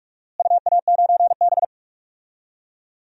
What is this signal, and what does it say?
Morse code, heard as a keyed tone, about 46 wpm, 700 Hz. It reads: UA9B
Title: DXCC-100 Call Signs - 4 Characters - Repeated 2x - 45wpm